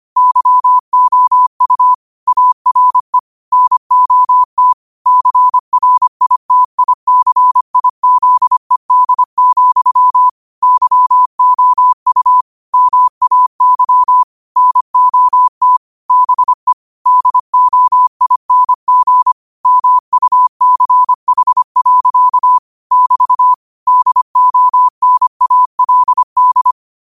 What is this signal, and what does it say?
Morse code, heard as a keyed tone, about 25 wpm, 1000 Hz. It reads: YOU ARE NOT CRITICIZED, YOU MAY NOT BE DOING MUCH. <BT> DONALD
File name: qotdfiles / 25wpm